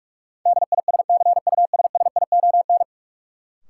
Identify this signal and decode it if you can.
Morse code, heard as a keyed tone, about 46 wpm, 700 Hz. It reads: DISCUSSION
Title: Sentences from Top 1100 Words - New Word Drill Alt - 45wpm